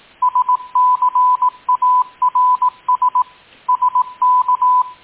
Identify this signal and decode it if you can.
Morse code, heard as a keyed tone, about 18 wpm, 1000 Hz. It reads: SCARS SK